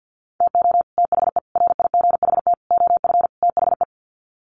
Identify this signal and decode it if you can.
Morse code, heard as a keyed tone, about 50 wpm, 700 Hz. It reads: TO THE RIGHT OF THE
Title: Sets of 5 Words Encore 50wpm